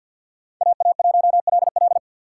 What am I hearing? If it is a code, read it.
Morse code, heard as a keyed tone, about 52 wpm, 700 Hz. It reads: AA1LL
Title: USA Call Signs - 5 Characters - 3x Character Spacing - 50wpm